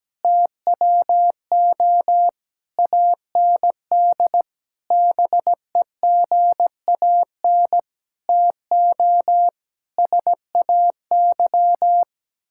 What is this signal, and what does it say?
Morse code, heard as a keyed tone, about 17 wpm, 700 Hz. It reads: TWO AND BEGAN TO SAY
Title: Sentences from Top 1000 Words - Review Part 5 - 17wpm